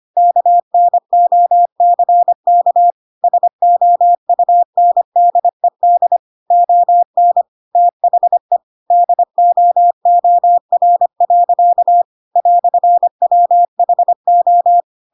Morse code, heard as a keyed tone, about 25 wpm, 700 Hz. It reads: KNOCK SOUNDED ON THE DOOR. "WHO